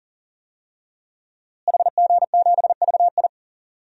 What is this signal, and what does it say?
Morse code, heard as a keyed tone, about 40 wpm, 700 Hz. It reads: HG7VI
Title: DXCC-100 Call Signs - 5 Characters - Repeated 3x - 40wpm